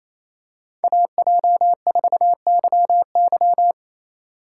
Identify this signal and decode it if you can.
Morse code, heard as a keyed tone, about 28 wpm, 700 Hz. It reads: AJ4YY
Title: USA Call Signs - 5 Characters - 4x Character Spacing - 28wpm